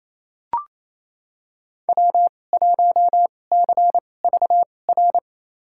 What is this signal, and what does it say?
Morse code, heard as a keyed tone, about 28 wpm, spaced as for 20 wpm, 700 Hz. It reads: W1CVR
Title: USA Call Signs - 5 Characters - 2x Character Spacing - 28wpm